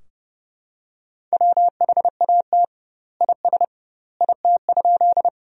Morse code, heard as a keyed tone, about 30 wpm, 700 Hz. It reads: WHAT IS IT?
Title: 2x Word-Spacing - Sentences from Top 100 Words - 30wpm